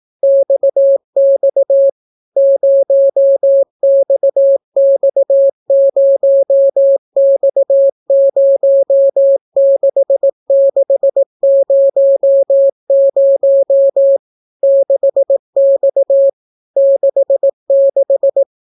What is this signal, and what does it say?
Morse code, heard as a keyed tone, about 18 wpm, 550 Hz. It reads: XX 0XX0X06600 6X 66